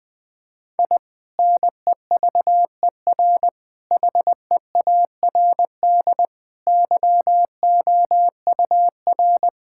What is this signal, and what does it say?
Morse code, heard as a keyed tone, about 20 wpm, 700 Hz. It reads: I NEVER HEARD YOUR
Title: Sentences from Top 1000 Words - Review Part 6 - 20wpm